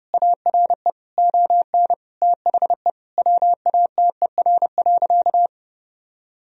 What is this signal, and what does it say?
Morse code, heard as a keyed tone, about 32 wpm, 700 Hz. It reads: ARE ON THE WATER.
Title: Sentences from Top 1000 Words - Review Part 1 - 30wpm